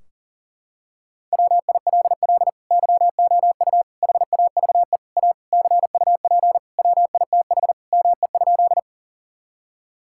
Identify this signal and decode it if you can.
Morse code, heard as a keyed tone, about 37 wpm, 700 Hz. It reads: WILL YOU HAVE A CUP WITH ME?
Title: Sentences from Top 1600 Words 40wpm